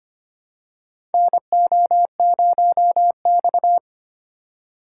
Morse code, heard as a keyed tone, about 25 wpm, 700 Hz. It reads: NO0X
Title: USA Call Signs - 4 Characters - 2x Character Spacing - 25wpm